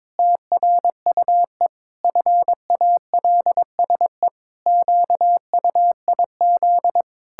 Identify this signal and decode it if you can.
Morse code, heard as a keyed tone, about 22 wpm, 700 Hz. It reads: TRUE FALSE QUIZ
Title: Pangram Challenge v2 - 22wpm